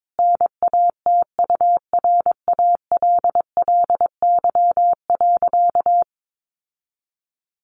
Morse code, heard as a keyed tone, about 22 wpm, 700 Hz. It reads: NATURALLY.
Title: Sentences from Top 1500 Words 22wpm